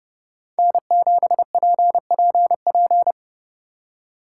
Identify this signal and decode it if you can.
Morse code, heard as a keyed tone, about 33 wpm, 700 Hz. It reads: N7PPP